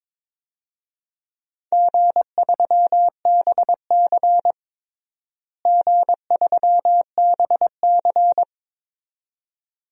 Morse code, heard as a keyed tone, about 22 wpm, 700 Hz. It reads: G3BC G3BC
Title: DXCC-100 Call Signs - 4 Characters - Repeated 2x - 22wpm